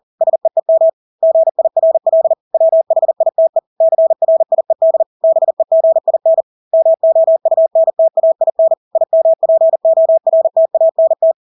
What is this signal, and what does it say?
Morse code, heard as a keyed tone, about 40 wpm, 650 Hz. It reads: SEEM GIRL WHITE CRIED BEGIN MOUNTAIN IMPORTANT
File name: Engelske ord 40-40 wpm